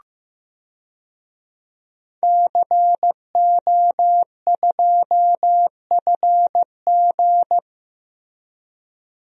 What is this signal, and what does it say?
Morse code, heard as a keyed tone, about 15 wpm, 700 Hz. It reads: CO2FG
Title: DXCC-100 Call Signs - 5 Characters - Repeated 2x - 15wpm